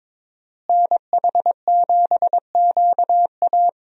Morse code, heard as a keyed tone, about 23 wpm, 700 Hz. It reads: NH7QA
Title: USA Call Signs - 5 Characters - Repeated 2x - 22wpm